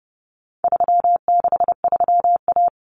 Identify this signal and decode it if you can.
Morse code, heard as a keyed tone, about 30 wpm, 700 Hz. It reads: 363A
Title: Common Cut Numbers - 4-Digit 30wpm